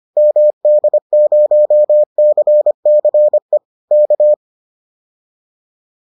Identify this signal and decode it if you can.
Morse code, heard as a keyed tone, about 25 wpm, 600 Hz. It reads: MD0CCE K